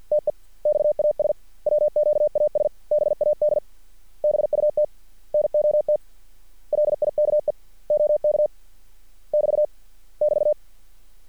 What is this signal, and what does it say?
Morse code, heard as a keyed tone, about 50 wpm, 600 Hz. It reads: TEXAS WQAS BAD BUT NOT LIKE OK <BT> <BT>
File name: archived / CRSnet